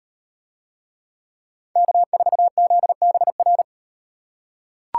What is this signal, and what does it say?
Morse code, heard as a keyed tone, about 38 wpm, 700 Hz. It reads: K4ZBR